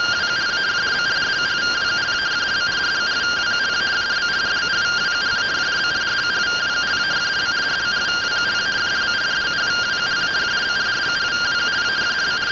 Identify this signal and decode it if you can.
Two-tone teleprinter signal, about 170 Hz shift.